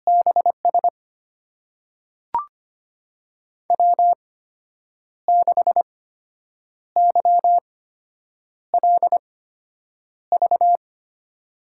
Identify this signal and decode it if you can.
Morse code, heard as a keyed tone, about 25 wpm, spaced as for 8 wpm, 700 Hz. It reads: BS W6YLV
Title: USA Call Signs - 5 Characters - 8x Character Spacing - 25wpm